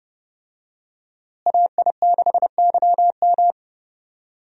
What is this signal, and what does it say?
Morse code, heard as a keyed tone, about 30 wpm, 700 Hz. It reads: AI6YM
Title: USA Call Signs - 5 Characters - Repeated 3x - 30wpm